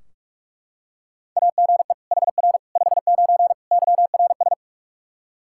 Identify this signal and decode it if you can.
Morse code, heard as a keyed tone, about 41 wpm, 700 Hz. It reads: AGE HR 59 YRS
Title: Practice QSO - Exchange 3 - More Info - 45wpm